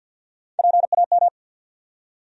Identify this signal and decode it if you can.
Morse code, heard as a keyed tone, about 53 wpm, 700 Hz. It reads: PAM
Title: USA Common Names - Max 5 Letters - 50wpm